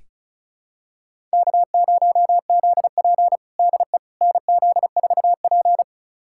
Morse code, heard as a keyed tone, about 36 wpm, 700 Hz. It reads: K0ZP DE NZ4P
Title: USA Callsign DE Callsign - 4 Characters - 35wpm